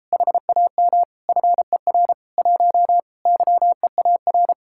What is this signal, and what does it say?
Morse code, heard as a keyed tone, about 35 wpm, 700 Hz. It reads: HAM FER 1 YEAR